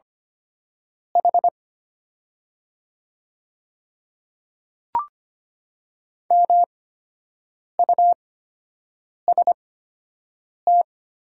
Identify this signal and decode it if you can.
Morse code, heard as a keyed tone, about 25 wpm, spaced as for 7 wpm, 690 Hz. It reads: H MUST
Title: 8x Character Spacing - Top 500 Words - 25wpm